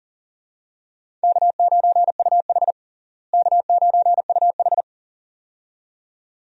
Morse code, heard as a keyed tone, about 40 wpm, 700 Hz. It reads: K9UH K9UH
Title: USA Call Signs - 4 Characters - Repeated 2x - 40wpm